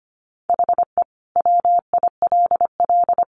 Morse code, heard as a keyed tone, about 25 wpm, 700 Hz. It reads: HE WILL
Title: Additional Easy Phrases 25wpm